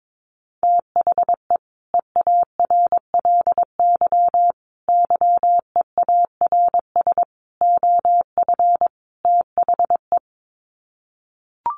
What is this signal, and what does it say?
Morse code, heard as a keyed tone, about 22 wpm, 700 Hz. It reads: THE EARLY YEARS OF THE
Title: Sets of 5 Words Encore 22wpm